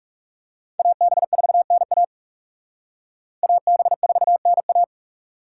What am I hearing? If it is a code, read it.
Morse code, heard as a keyed tone, about 37 wpm, 700 Hz. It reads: AB4NA AB4NA
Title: Speed Racing - USA 2x2 Call Signs 30wpm